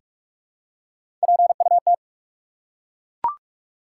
Morse code, heard as a keyed tone, about 42 wpm, 700 Hz. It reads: PUT
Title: Three Letter Words - 45wpm